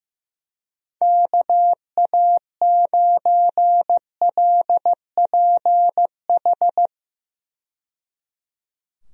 Morse code, heard as a keyed tone, about 15 wpm, 700 Hz. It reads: KA9LPH
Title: USA Call Signs - 6 Characters - 15wpm